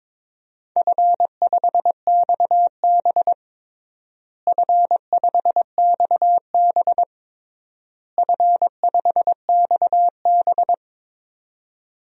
Morse code, heard as a keyed tone, about 22 wpm, 700 Hz. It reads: F5XB F5XB F5XB
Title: DXCC-100 Call Signs - 4 Characters - Repeated 3x - 22wpm